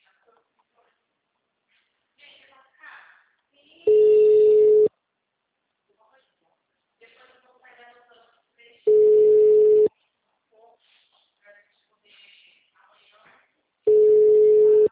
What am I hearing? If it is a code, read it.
Ringback tone (Europe, 425 Hz), with speech.
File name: out- -2011- - - .96